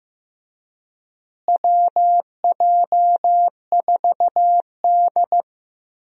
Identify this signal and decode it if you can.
Morse code, heard as a keyed tone, about 14 wpm, 700 Hz. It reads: WJ4D
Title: USA Call Signs - 4 Characters - Repeated 3x - 15wpm